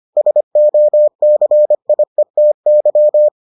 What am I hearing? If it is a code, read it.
Morse code, heard as a keyed tone, about 25 wpm, 600 Hz. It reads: SOCIETY